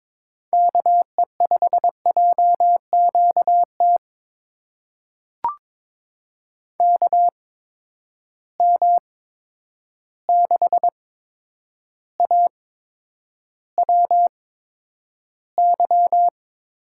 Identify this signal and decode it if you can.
Morse code, heard as a keyed tone, about 22 wpm, spaced as for 9 wpm, 700 Hz. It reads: KE5JQT KM6AWY
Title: USA Call Signs - 6 Characters - 8x Character Spacing - 22wpm